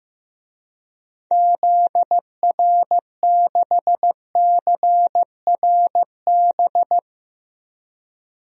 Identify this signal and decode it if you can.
Morse code, heard as a keyed tone, about 15 wpm, 700 Hz. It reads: ZR6CRB
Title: DXCC-100 Call Signs - 6 Characters - 15wpm